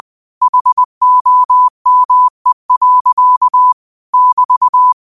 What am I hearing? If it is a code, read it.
Morse code, heard as a keyed tone, about 20 wpm, 1000 Hz. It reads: HOME. <BT>